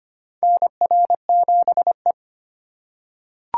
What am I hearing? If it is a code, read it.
Morse code, heard as a keyed tone, about 25 wpm, 700 Hz. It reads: NR7E
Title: USA Call Signs - 4 Characters - 3x Character Spacing - 25wpm